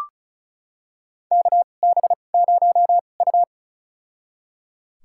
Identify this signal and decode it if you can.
Morse code, heard as a keyed tone, about 36 wpm, spaced as for 25 wpm, 700 Hz. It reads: KB0U
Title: USA Call Signs - 4 Characters - 2x Character Spacing - 35wpm